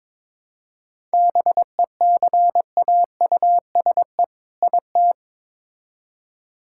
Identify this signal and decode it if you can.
Morse code, heard as a keyed tone, about 22 wpm, 700 Hz. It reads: BECAUSE IT
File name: Build A Phrase - 2-Word Phrases - 22wpm